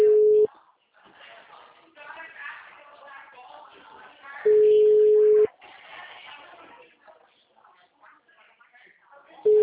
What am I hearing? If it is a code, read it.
Ringback tone (Europe, 425 Hz), with speech.